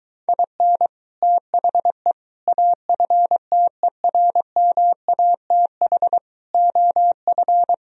Morse code, heard as a keyed tone, about 23 wpm, 700 Hz. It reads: IN THE AFTERMATH OF